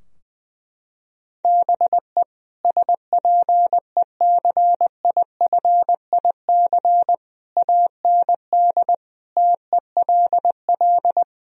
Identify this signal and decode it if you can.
Morse code, heard as a keyed tone, about 20 wpm, 700 Hz. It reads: BE SPECIFIC AND TELL
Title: Sentences from Top 1000 Words - Review Part 3 - 20wpm